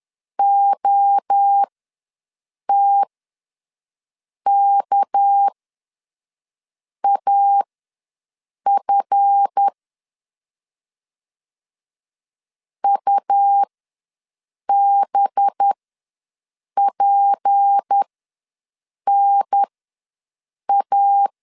Morse code, heard as a keyed tone, about 10 wpm, spaced as for 5 wpm, 800 Hz. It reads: OTKAF UBPNA
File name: LEZIONI / 8 YJ45 / 8 YJ4 ws 10wpm 4020